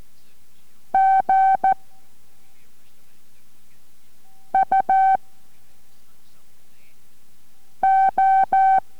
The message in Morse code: G U O